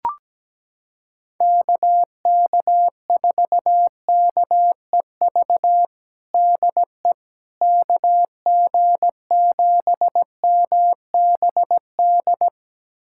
Morse code, heard as a keyed tone, about 17 wpm, 700 Hz. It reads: KK4KEV DE KG7MBD
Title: USA Callsign DE Callsign - 6 Characters - 17wpm